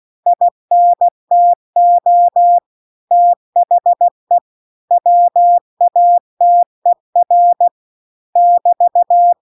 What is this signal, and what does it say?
Morse code, heard as a keyed tone, about 16 wpm, 700 Hz. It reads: INTO THE WATER <BT>
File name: CW-The-Big-Twoheaded-River-2-16WPM